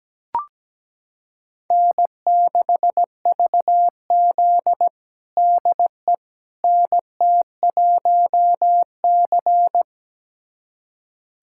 Morse code, heard as a keyed tone, about 17 wpm, 700 Hz. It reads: N6VZ DE NT1C
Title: USA Callsign DE Callsign - 4 Characters - 17wpm